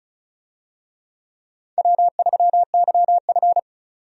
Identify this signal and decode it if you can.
Morse code, heard as a keyed tone, about 36 wpm, 700 Hz. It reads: W3YF
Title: USA Call Signs - 4 Characters - 35wpm